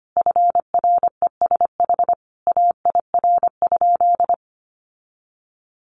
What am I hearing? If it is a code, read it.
Morse code, heard as a keyed tone, about 25 wpm, 700 Hz. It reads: FRESH AIR?
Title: Sentences from Top 1200 Words 25wpm